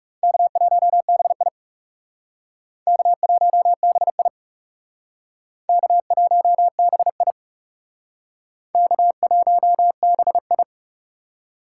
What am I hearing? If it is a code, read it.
Morse code, heard as a keyed tone, about 40 wpm, 700 Hz. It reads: K1BI K1BI K1BI K1BI